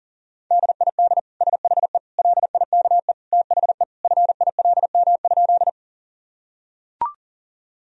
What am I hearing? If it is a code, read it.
Morse code, heard as a keyed tone, about 37 wpm, 700 Hz. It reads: DID SHE LIKE THE FILM?
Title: Sentences from Top 1000 Words - Review Part 3 - 40wpm